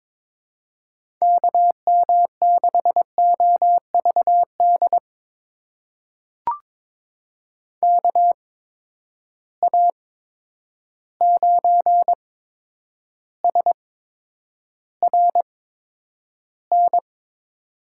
Morse code, heard as a keyed tone, about 22 wpm, spaced as for 9 wpm, 700 Hz. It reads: KM6OVD KA9SRN
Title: USA Call Signs - 6 Characters - 8x Character Spacing - 22wpm